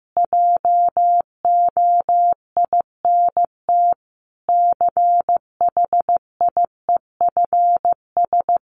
Morse code, heard as a keyed tone, about 14 wpm, 700 Hz. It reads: JOINT CHIEFS
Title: Sets of 5 Words Encore 15wpm